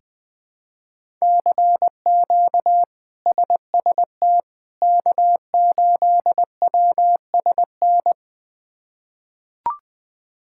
Morse code, heard as a keyed tone, about 20 wpm, 700 Hz. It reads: CQ SST K8WSN
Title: K1USN SST QSO Practice - 20wpm